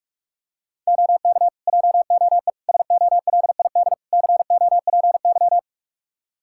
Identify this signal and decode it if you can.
Morse code, heard as a keyed tone, about 41 wpm, 700 Hz. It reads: OK JOE SOLID COPY
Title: Practice QSO - Exchange 3 - More Info - 45wpm